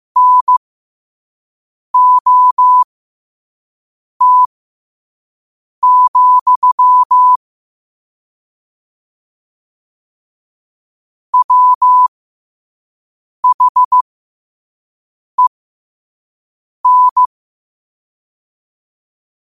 Morse code, heard as a keyed tone, about 15 wpm, spaced as for 5 wpm, 1000 Hz. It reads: NOT, WHEN
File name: 5wpm